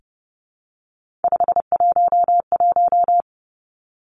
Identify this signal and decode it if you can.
Morse code, heard as a keyed tone, about 30 wpm, 700 Hz. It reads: 511
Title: Three Challenging Letter-Number Combinations 30wpm